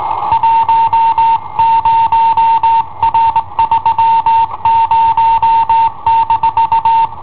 Morse code, heard as a keyed tone, about 19 wpm, 910 Hz. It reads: 10R30-
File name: i7ohp / cassetta i7ohp B